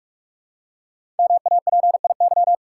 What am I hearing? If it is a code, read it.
Morse code, heard as a keyed tone, about 44 wpm, 700 Hz. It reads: MAPIY
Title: Code Groups - 5 Letters - 45wpm